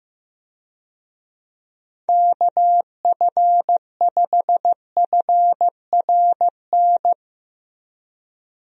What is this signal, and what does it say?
Morse code, heard as a keyed tone, about 15 wpm, 700 Hz. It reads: KF5FRN
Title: USA Call Signs - 6 Characters - Repeated 2x - 15wpm